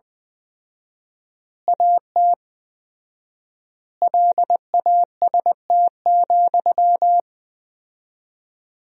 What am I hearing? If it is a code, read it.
Morse code, heard as a keyed tone, about 20 wpm, 700 Hz. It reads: AT LAST,
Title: 4x Word-Spacing - Sentences from Top 300 Words - 20wpm